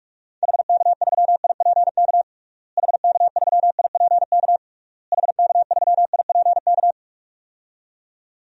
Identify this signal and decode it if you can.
Morse code, heard as a keyed tone, about 41 wpm, 700 Hz. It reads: HK3IPK HK3IPK HK3IPK
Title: DXCC-100 Call Signs - 6 Characters - Repeated 3x - 45wpm